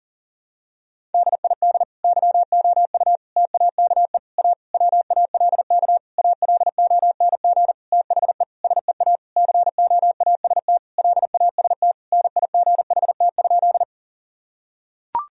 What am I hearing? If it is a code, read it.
Morse code, heard as a keyed tone, about 37 wpm, 700 Hz. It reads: DID YOU TAKE A WALK ALONG THE SEA COAST LAST NIGHT?
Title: Sentences from Top 1000 Words - Review Part 4 - 40wpm